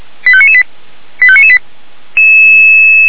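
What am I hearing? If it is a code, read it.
Five-tone selective call (ZVEI), heard as quick runs of five tones, tones of about 70 ms. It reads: 85998 85998
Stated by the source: ZVEI five-tone selective call received by SDR.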